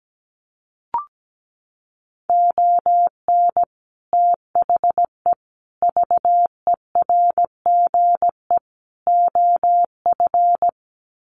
Morse code, heard as a keyed tone, about 17 wpm, 700 Hz. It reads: ON THE VERGE OF
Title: Sets of 4 Words 17wpm